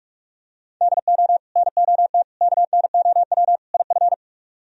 Morse code, heard as a keyed tone, about 49 wpm, 700 Hz. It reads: DO NOT KNOW IF